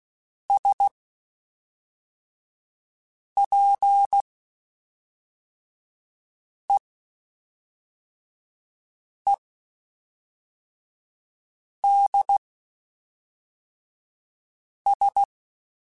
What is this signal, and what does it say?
Morse code, heard as a keyed tone, about 16 wpm, spaced as for 3 wpm, 800 Hz. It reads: SPEEDS
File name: Spaced Characters / Spaced Characters-mp3s / Weather 4 21 sp @5